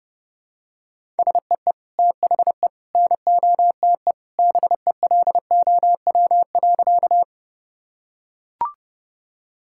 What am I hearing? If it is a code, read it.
Morse code, heard as a keyed tone, about 30 wpm, 700 Hz. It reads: SEE THE NOTE BELOW.
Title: Sentences from Top 1000 Words - Review Part 5 - 30wpm